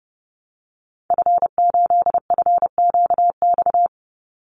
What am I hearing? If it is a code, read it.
Morse code, heard as a keyed tone, about 30 wpm, 700 Hz. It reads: F8FQX
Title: Call Signs Repeated 3x - 30wpm